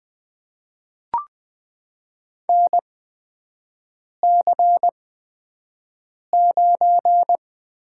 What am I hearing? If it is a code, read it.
Morse code, heard as a keyed tone, about 21 wpm, spaced as for 5 wpm, 700 Hz. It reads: NC9
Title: USA Call Signs - 4 Characters - 8x Character Spacing - 20wpm